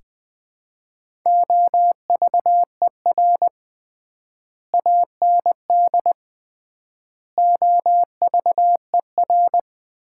Morse code, heard as a keyed tone, about 21 wpm, 700 Hz. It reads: OVER AND OVER
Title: 3x Word-Spacing - Sets of 4 Words - 20wpm